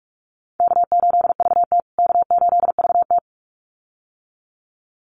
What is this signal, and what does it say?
Morse code, heard as a keyed tone, about 44 wpm, 700 Hz. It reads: K8VT K8VT
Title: USA Call Signs Repeated 2x 45wpm